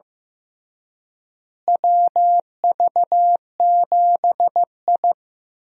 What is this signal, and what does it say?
Morse code, heard as a keyed tone, about 15 wpm, 700 Hz. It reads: WV7I
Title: USA Call Signs - 4 Characters - Repeated 3x - 15wpm